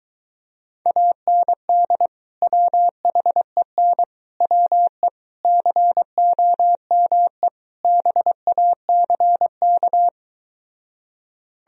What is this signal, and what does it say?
Morse code, heard as a keyed tone, about 23 wpm, 700 Hz. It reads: AND WHEN WE COME BACK